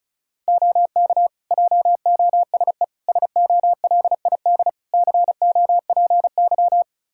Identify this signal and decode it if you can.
Morse code, heard as a keyed tone, about 36 wpm, 700 Hz. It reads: OK JOSE SOLID COPY
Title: K1USN SST QSO Practice 35wpm